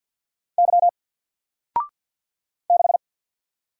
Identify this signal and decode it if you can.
Morse code, heard as a keyed tone, about 50 wpm, 700 Hz. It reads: Y 6
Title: Single Character (Letter:Number:Punctuation:Prosign) 50wpm